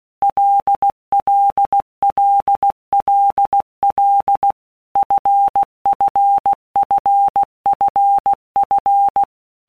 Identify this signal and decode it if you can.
Morse code, heard as a keyed tone, about 16 wpm, 800 Hz. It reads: LLLLL FFFFF
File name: Basic Letter Groups By Blocks / Groups@15 / LFCYQ@15